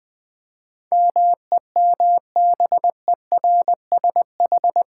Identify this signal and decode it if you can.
Morse code, heard as a keyed tone, about 20 wpm, 700 Hz. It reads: MEMBERSH
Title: Build A Word - 4 Letters At A Time - 10-Letter Words - 20wpm